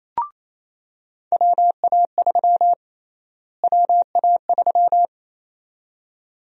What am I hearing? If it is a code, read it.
Morse code, heard as a keyed tone, about 28 wpm, 700 Hz. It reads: WA3 WA3
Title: USA Call Sign Prefixes - Repeated 2x - 28wpm